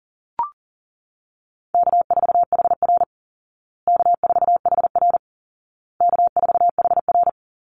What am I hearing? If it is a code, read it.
Morse code, heard as a keyed tone, about 40 wpm, 700 Hz. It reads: K4HR K4HR K4HR
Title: Easy Call Signs Repeated 3x - 40wpm